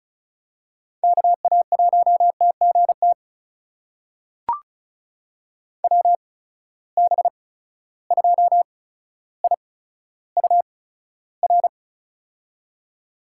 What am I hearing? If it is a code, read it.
Morse code, heard as a keyed tone, about 36 wpm, spaced as for 15 wpm, 700 Hz. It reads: KA1TGT WB2IUR
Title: USA Call Signs - 8x Character Spacing - 35wpm